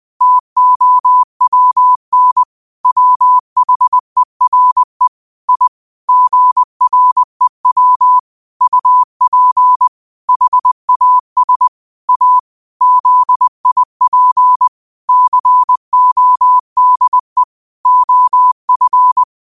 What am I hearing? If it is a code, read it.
Morse code, heard as a keyed tone, about 20 wpm, 1000 Hz. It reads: TOWN WHERE I GREW UP HAS A ZIP CODE OF
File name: qotdfiles / 20wpm